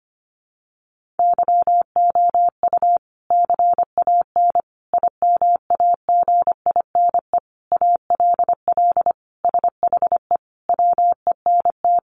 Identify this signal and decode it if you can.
Morse code, heard as a keyed tone, about 25 wpm, 700 Hz. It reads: YOU CAN IMAGINE ALL SHE WENT
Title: Sentences from Top 1400 Words 25wpm